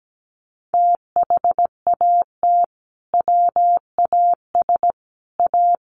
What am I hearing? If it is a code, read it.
Morse code, heard as a keyed tone, about 17 wpm, 700 Hz. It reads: THAT WAS A
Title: Sentences from Top 1200 Words 17wpm